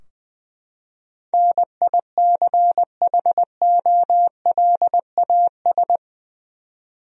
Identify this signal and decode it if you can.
Morse code, heard as a keyed tone, about 20 wpm, 700 Hz. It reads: NICHOLAS
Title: USA Common Names - 20wpm